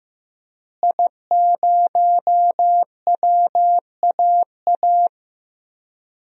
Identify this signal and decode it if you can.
Morse code, heard as a keyed tone, about 15 wpm, 700 Hz. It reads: I0WAA
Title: DXCC-100 Call Signs - 5 Characters - Repeated 2x - 15wpm